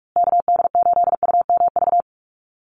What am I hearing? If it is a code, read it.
Morse code, heard as a keyed tone, about 44 wpm, 700 Hz. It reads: KD8UMV
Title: USA Call Signs Repeated 2x 45wpm